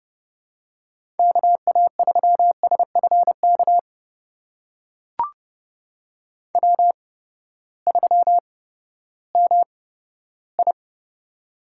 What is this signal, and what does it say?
Morse code, heard as a keyed tone, about 30 wpm, spaced as for 15 wpm, 700 Hz. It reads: KA3SFK W3MI
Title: USA Call Signs - 8x Character Spacing - 30wpm